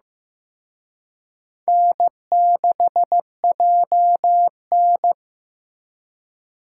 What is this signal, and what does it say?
Morse code, heard as a keyed tone, about 18 wpm, 700 Hz. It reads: N6JN
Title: Speed Racing - USA Call Signs 15wpm